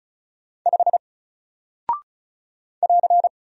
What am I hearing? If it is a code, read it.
Morse code, heard as a keyed tone, about 36 wpm, 700 Hz. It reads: 5 <AR>
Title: Single Character (Letter:Number:Punctuation:Prosign) Alt 35wpm